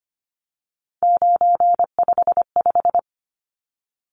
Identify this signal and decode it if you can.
Morse code, heard as a keyed tone, about 25 wpm, 700 Hz. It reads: 955
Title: Numbers - 3-Digit 25wpm